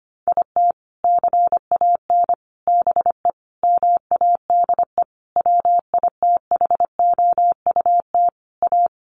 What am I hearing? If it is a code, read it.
Morse code, heard as a keyed tone, about 25 wpm, 700 Hz. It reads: IT CAN BE MADE WITHOUT A
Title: Sentences from Top 1200 Words 25wpm